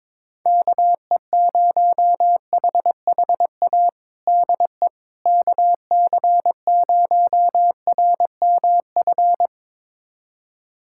Morse code, heard as a keyed tone, about 22 wpm, 700 Hz. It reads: KE0HHA DE KC0RMF
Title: USA Callsign DE Callsign 22wpm